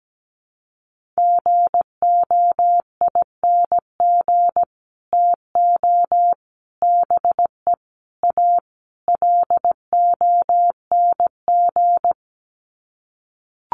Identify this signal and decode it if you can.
Morse code, heard as a keyed tone, about 17 wpm, 700 Hz. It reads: GOING TO BE A LONG
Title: Sets of 5 Words Encore2 17wpm